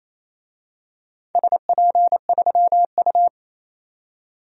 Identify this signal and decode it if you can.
Morse code, heard as a keyed tone, about 27 wpm, 700 Hz. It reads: SP3U
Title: DXCC-100 Call Signs - 4 Characters - 28wpm